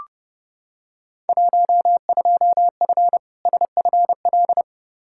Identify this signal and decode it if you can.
Morse code, heard as a keyed tone, about 30 wpm, 700 Hz. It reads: 12F SFL
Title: Field Day Exchanges Encore - 30wpm